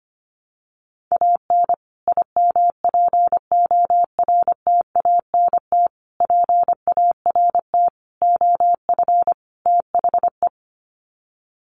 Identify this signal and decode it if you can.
Morse code, heard as a keyed tone, about 25 wpm, 700 Hz. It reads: AN IMPORTANT PART OF THE
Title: Sets of 5 Words Encore 25wpm